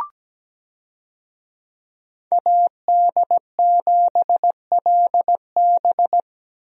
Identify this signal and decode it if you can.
Morse code, heard as a keyed tone, about 17 wpm, 700 Hz. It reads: AD7LB
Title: USA Call Signs - 5 Characters - Repeated 3x - 17wpm